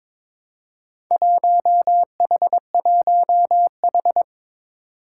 Morse code, heard as a keyed tone, about 22 wpm, 700 Hz. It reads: 1H1H
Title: Four Challenging Letter-Number Combinations 22wpm